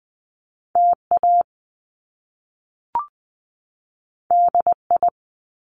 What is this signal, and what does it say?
Morse code, heard as a keyed tone, about 20 wpm, 700 Hz. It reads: TA DI
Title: Top Two Letter Combinations in Words 20wpm